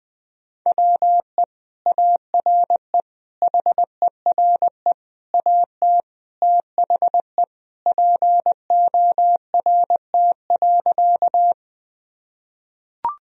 Morse code, heard as a keyed tone, about 20 wpm, 700 Hz. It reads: WE ARE HERE AT THE PORT.
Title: Sentences from Top 1000 Words - Review Part 1 - 20wpm